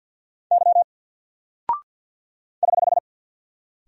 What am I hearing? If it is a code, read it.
Morse code, heard as a keyed tone, about 50 wpm, 700 Hz. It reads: Y <HH>
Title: Single Character (Letter:Number:Punctuation:Prosign) 50wpm